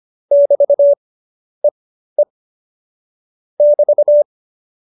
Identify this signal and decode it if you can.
Morse code, heard as a keyed tone, about 25 wpm, spaced as for 11 wpm, 570 Hz. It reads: <BT>EE <BT>